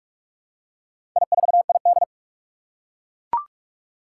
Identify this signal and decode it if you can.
Morse code, heard as a keyed tone, about 44 wpm, 700 Hz. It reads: I4ID
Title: Four Challenging Letter-Number Combinations 45wpm